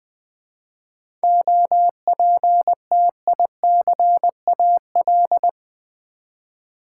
Morse code, heard as a keyed tone, about 20 wpm, 700 Hz. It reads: OPTICAL
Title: 7 Letter Words - First 3 Letters Given - 20wpm